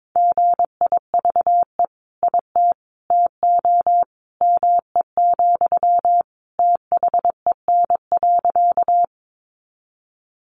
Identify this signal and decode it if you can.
Morse code, heard as a keyed tone, about 22 wpm, 700 Hz. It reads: GIVE IT TO ME, THEN.
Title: Sentences from Top 200 Words 22wpm